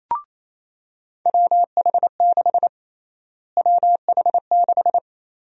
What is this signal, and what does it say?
Morse code, heard as a keyed tone, about 27 wpm, 700 Hz. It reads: WH6 WH6
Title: USA Call Sign Prefixes - Repeated 2x - 28wpm